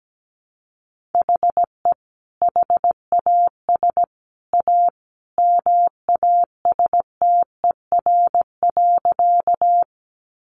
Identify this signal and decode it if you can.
Morse code, heard as a keyed tone, about 17 wpm, 700 Hz. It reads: HE HAS A MASTER.
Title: Sentences from Top 1300 Words 17wpm